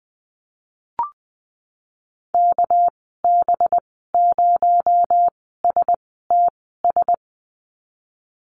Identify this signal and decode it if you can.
Morse code, heard as a keyed tone, about 20 wpm, spaced as for 14 wpm, 700 Hz. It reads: KB0STS
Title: 245 Lesson 39g - Letter z - Call Signs 20wpm